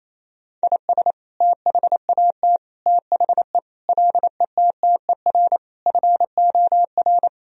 Morse code, heard as a keyed tone, about 27 wpm, 700 Hz. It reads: IS THAT THE LETTER FOR
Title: Sentences from Top 1000 Words - Review Part 2 - 28wpm